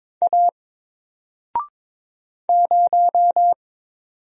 Morse code, heard as a keyed tone, about 22 wpm, 700 Hz. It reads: A 0
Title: Single Letter-Number Alt 22wpm